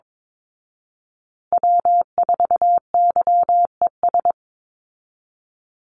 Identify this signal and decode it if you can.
Morse code, heard as a keyed tone, about 22 wpm, 700 Hz. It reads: W4YES
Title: Call Signs Repeated 3x - 22wpm